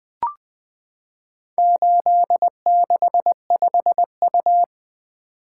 Morse code, heard as a keyed tone, about 21 wpm, 700 Hz. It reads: 865U
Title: Four Letter-Number Combinations - 20wpm